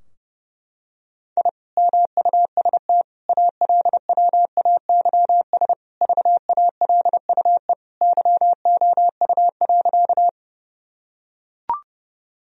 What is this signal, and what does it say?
Morse code, heard as a keyed tone, about 32 wpm, 700 Hz. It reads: I MUST ALWAYS VALUE YOU.
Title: Sentences from Top 1000 Words - Review Part 3 - 30wpm